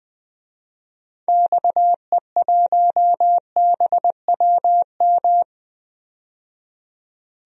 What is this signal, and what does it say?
Morse code, heard as a keyed tone, about 20 wpm, 700 Hz. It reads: XE1BWM
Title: DXCC-100 Call Signs - 6 Characters - 20wpm